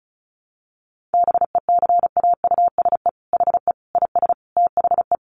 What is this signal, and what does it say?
Morse code, heard as a keyed tone, about 34 wpm, 700 Hz. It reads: BECAUSE HE IS THE
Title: Sentences from Top 1500 Words 35wpm